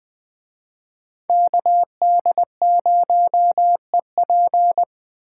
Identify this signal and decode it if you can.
Morse code, heard as a keyed tone, about 20 wpm, 700 Hz. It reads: KD0EP
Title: Speed Racing - USA 2x2 Call Signs 17wpm